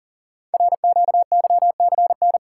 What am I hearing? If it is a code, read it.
Morse code, heard as a keyed tone, about 38 wpm, 690 Hz. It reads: RQYCN
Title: Code Groups - 5 Letters - 40wpm